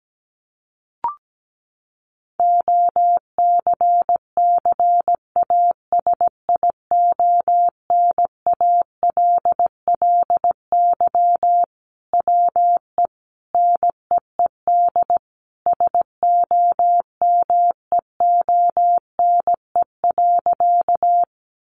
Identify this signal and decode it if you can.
Morse code, heard as a keyed tone, about 17 wpm, 700 Hz. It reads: OCCASIONALLY WE NEED SOMEONE.
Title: Sentences from Top 1300 Words 17wpm